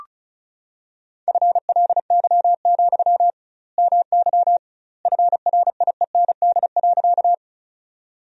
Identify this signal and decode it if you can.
Morse code, heard as a keyed tone, about 36 wpm, 700 Hz. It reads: FLY, MY FRIEND.
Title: 2x Word-Spacing - Sentences from Top 500 Words - 35wpm